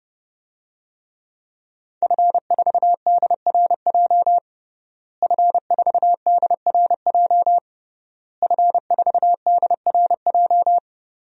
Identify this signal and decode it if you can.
Morse code, heard as a keyed tone, about 32 wpm, 700 Hz. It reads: F4DRJ F4DRJ F4DRJ
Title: DXCC-100 Call Signs - 5 Characters - Repeated 3x - 30wpm